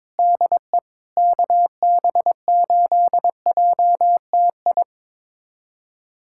Morse code, heard as a keyed tone, about 22 wpm, 700 Hz. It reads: DE KB8JTI
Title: USA Callsign DE Callsign 22wpm